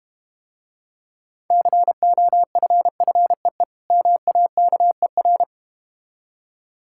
Morse code, heard as a keyed tone, about 33 wpm, 700 Hz. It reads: COFFEE MAKER